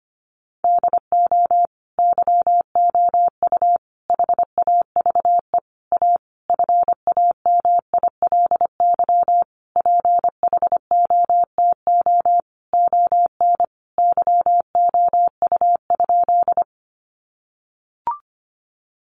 Morse code, heard as a keyed tone, about 25 wpm, 700 Hz. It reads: DO YOU HAVE A FAMILY PHOTO ON YOU?
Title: Sentences from Top 1200 Words 25wpm